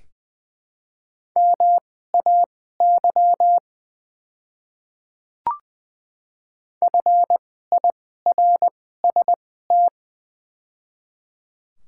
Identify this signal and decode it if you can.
Morse code, heard as a keyed tone, about 20 wpm, spaced as for 14 wpm, 700 Hz. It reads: MAY FIRST
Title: 2x Character Spacing - Top 200 Words - 20wpm